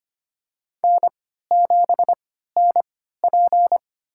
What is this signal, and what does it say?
Morse code, heard as a keyed tone, about 25 wpm, spaced as for 14 wpm, 700 Hz. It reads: N7NP
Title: USA Call Signs - 3x Character Spacing - 25wpm